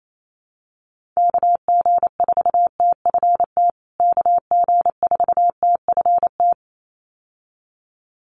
Morse code, heard as a keyed tone, about 28 wpm, 700 Hz. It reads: KG4TFT KG4TFT
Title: USA Call Signs Repeated 2x 28wpm